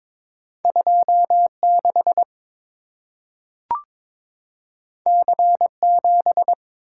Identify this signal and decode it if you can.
Morse code, heard as a keyed tone, about 23 wpm, 690 Hz. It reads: 26 C7
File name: practice / 22wpm / Two Letter-Number Combinations - 22wpm